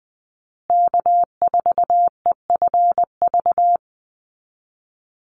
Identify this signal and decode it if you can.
Morse code, heard as a keyed tone, about 20 wpm, 700 Hz. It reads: K4EFV
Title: USA Call Signs Repeated 3x 20wpm